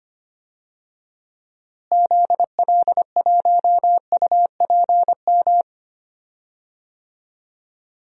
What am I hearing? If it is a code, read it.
Morse code, heard as a keyed tone, about 25 wpm, 700 Hz. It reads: ZL1UPM
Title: DXCC-100 Call Signs - 6 Characters - Repeated 3x - 25wpm